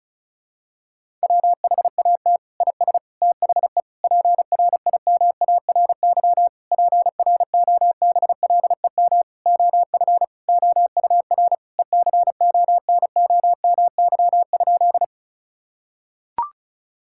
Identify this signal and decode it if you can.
Morse code, heard as a keyed tone, about 36 wpm, 700 Hz. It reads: WHAT IS THE PRIMARY PROBLEM OF OUR ECONOMY?
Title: Sentences from Top 1000 Words - Review Part 5 - 35wpm